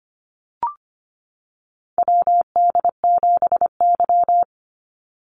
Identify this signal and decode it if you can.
Morse code, heard as a keyed tone, about 25 wpm, 700 Hz. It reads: WD7Y
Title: NA SOTA Chaser Callsigns 25wpm